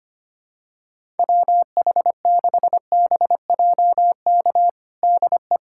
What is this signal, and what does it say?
Morse code, heard as a keyed tone, about 25 wpm, 700 Hz. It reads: WH6BJK DE
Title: USA Callsign DE Callsign 25wpm